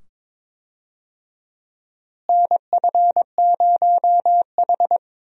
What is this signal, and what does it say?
Morse code, heard as a keyed tone, about 22 wpm, 700 Hz. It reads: NF0H
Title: USA Call Signs - 4 Characters - Repeated 2x - 22wpm